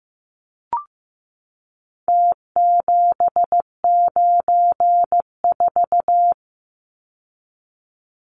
Morse code, heard as a keyed tone, about 15 wpm, 700 Hz. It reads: T794
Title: Numbers - Contest Style - 4-Digit 15wpm